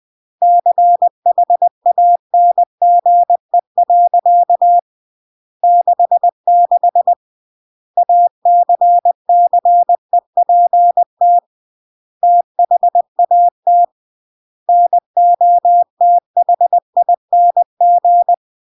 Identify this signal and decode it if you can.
Morse code, heard as a keyed tone, about 20 wpm, 700 Hz. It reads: CHANGE. 66 ACCEPT THAT NOTHING
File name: CW / jodorowsky / 82 commandments 20-20